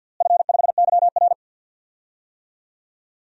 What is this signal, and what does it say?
Morse code, heard as a keyed tone, about 50 wpm, 700 Hz. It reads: F5YR
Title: DXCC-100 Call Signs - 4 Characters - Repeated 2x - 50wpm